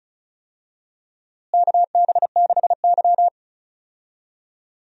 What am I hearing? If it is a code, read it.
Morse code, heard as a keyed tone, about 34 wpm, 700 Hz. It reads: KB6Y
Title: USA Call Signs - 4 Characters - Repeated 3x - 35wpm